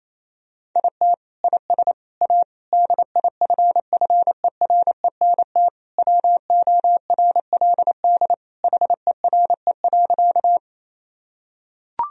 Morse code, heard as a keyed tone, about 28 wpm, 700 Hz. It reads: IT IS A DIFFERENT WORLD HERE.
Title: Sentences from Top 1000 Words - Review Part 3 - 28wpm